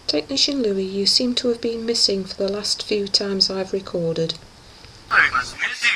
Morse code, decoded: none